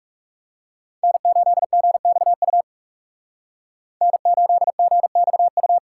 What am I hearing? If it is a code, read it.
Morse code, heard as a keyed tone, about 38 wpm, 700 Hz. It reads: N8GXU N8GXU